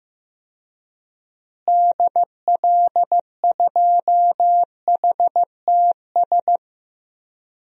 Morse code, heard as a keyed tone, about 15 wpm, 700 Hz. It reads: DL2HTS
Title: DXCC-100 Call Signs - 6 Characters - Repeated 3x - 15wpm